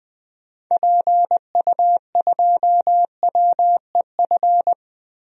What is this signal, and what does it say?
Morse code, heard as a keyed tone, about 20 wpm, 700 Hz. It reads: PU2WEF
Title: DXCC-100 Call Signs - 6 Characters - Repeated 3x - 20wpm